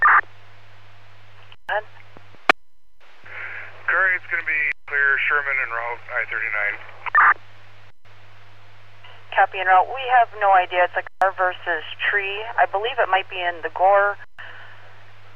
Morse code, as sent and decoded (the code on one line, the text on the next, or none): none